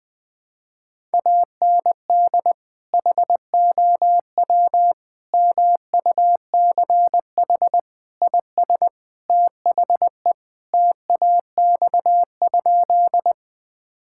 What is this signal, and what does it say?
Morse code, heard as a keyed tone, about 20 wpm, 700 Hz. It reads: AND HOW MUCH IS THE TAX?
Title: Sentences from Top 1000 Words - Review Part 5 - 20wpm